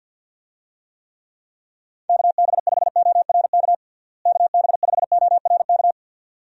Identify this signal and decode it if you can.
Morse code, heard as a keyed tone, about 51 wpm, 700 Hz. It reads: KB5ORK KB5ORK
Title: USA Call Signs - 6 Characters - Repeated 2x - 50wpm